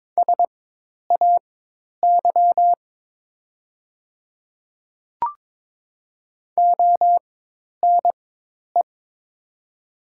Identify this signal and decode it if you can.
Morse code, heard as a keyed tone, about 22 wpm, spaced as for 10 wpm, 700 Hz. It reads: SAY ONE
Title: 4x Character Spacing - Top 300 Words - 22wpm